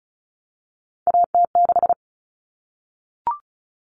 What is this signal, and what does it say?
Morse code, heard as a keyed tone, about 34 wpm, 700 Hz. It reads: AT6
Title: Common Cut Numbers - 3-Digit 35wpm